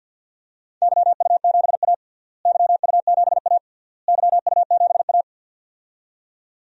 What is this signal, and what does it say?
Morse code, heard as a keyed tone, about 52 wpm, 700 Hz. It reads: YU7A YU7A YU7A
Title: DXCC-100 Call Signs - 4 Characters - Repeated 3x - 50wpm